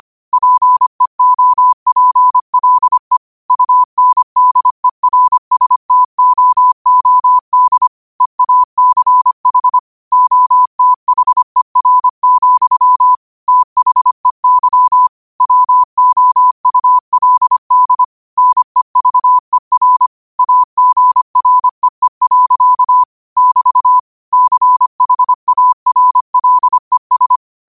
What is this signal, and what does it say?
Morse code, heard as a keyed tone, about 25 wpm, 1000 Hz. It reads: PEOPLE UNDERSTOOD EACH OTHER, THEY WOULD NEVER AGREE. <BT> CHARLES